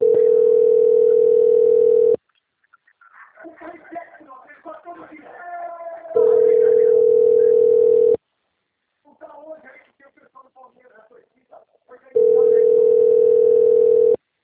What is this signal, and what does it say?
Ringback tone (North America), 440 + 480 Hz, with speech.
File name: out- - - - - .34487